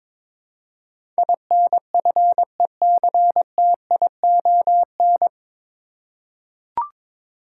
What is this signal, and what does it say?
Morse code, heard as a keyed tone, about 21 wpm, 700 Hz. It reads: INFECTION
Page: Morse Code Ninja practice 22wpm